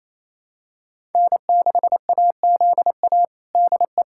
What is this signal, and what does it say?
Morse code, heard as a keyed tone, about 28 wpm, 700 Hz. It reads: N6AZA DE
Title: USA Callsign DE Callsign - 5 Characters - 28wpm